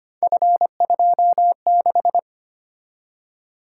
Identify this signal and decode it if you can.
Morse code, heard as a keyed tone, about 25 wpm, 700 Hz. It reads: F26
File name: Three Letter-Number Combinations - 25wpm